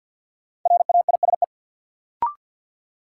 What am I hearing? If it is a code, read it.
Morse code, heard as a keyed tone, about 53 wpm, 700 Hz. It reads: RAISE
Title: Sentences from Top 1400 Words - New Word Drill - 50wpm